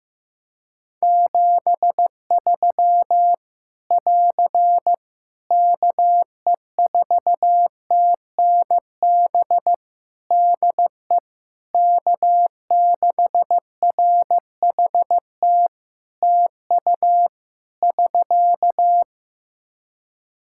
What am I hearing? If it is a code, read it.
Morse code, heard as a keyed tone, about 15 wpm, 700 Hz. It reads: 73 <AR> KE4TNB DE K6RHT TU <SK>
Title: Practice QSO - Exchange 4 - Ending - 15wpm